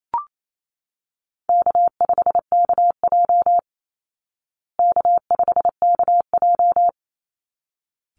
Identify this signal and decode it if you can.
Morse code, heard as a keyed tone, about 28 wpm, 700 Hz. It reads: K5KJ K5KJ
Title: Call Signs Repeated 3x - 28wpm